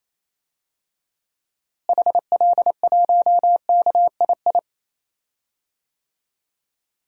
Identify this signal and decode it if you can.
Morse code, heard as a keyed tone, about 28 wpm, 700 Hz. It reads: HL1KII
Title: DXCC-100 Call Signs - 6 Characters - Repeated 2x - 28wpm